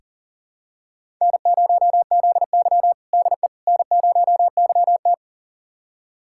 Morse code, heard as a keyed tone, about 40 wpm, 700 Hz. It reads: N0ZY DE N0YT
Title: USA Callsign DE Callsign - 4 Characters - 40wpm